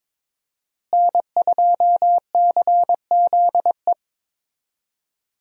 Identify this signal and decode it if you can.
Morse code, heard as a keyed tone, about 22 wpm, 700 Hz. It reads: N2CZE
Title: USA Call Signs - 5 Characters - Repeated 2x - 22wpm